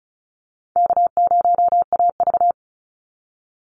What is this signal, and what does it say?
Morse code, heard as a keyed tone, about 34 wpm, 700 Hz. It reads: K0AV
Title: Easy Call Signs Repeated 3x - 35wpm